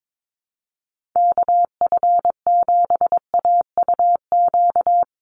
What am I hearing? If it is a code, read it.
Morse code, heard as a keyed tone, about 22 wpm, 700 Hz. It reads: KF7AUQ
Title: USA Call Signs 22wpm